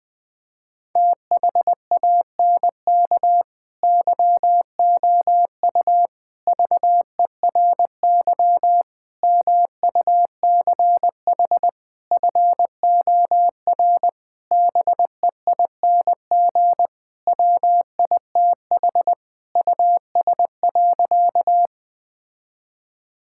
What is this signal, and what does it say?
Morse code, heard as a keyed tone, about 20 wpm, 700 Hz. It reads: THANK YOU VERY MUCH FOR BEING WITH US.
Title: Sentences with 5-Word Phrases - 20wpm